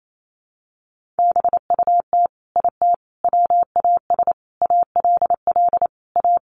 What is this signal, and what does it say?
Morse code, heard as a keyed tone, about 28 wpm, 700 Hz. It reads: BUT IT WAS ALL A
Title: Sentences from Top 1500 Words 28wpm